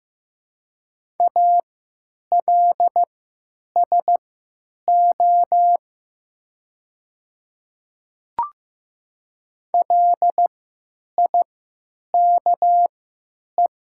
Morse code, heard as a keyed tone, about 15 wpm, spaced as for 9 wpm, 700 Hz. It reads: ALSO LIKE
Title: 3x Character Spacing - Top 300 Words - 15wpm